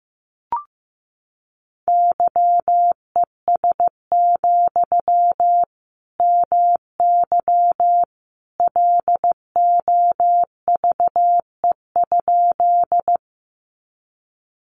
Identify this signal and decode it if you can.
Morse code, heard as a keyed tone, about 15 wpm, 700 Hz. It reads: YES, MY LOVE?
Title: Sentences from Top 500 Words 15wpm